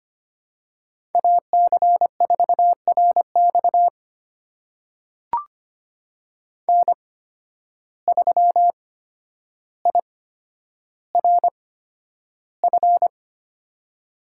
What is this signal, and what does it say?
Morse code, heard as a keyed tone, about 25 wpm, spaced as for 11 wpm, 700 Hz. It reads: AC4RX N3IRF
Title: USA Call Signs - 8x Character Spacing - 25wpm